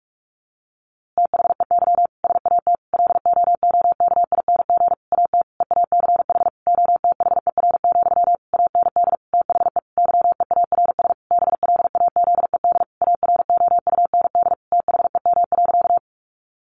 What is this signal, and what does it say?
Morse code, heard as a keyed tone, about 44 wpm, 700 Hz. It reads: THEY SAT LOOKING AT EACH OTHER, AND THE YEARS BLAZED AROUND THEM.
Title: A Sound Of Thunder 45wpm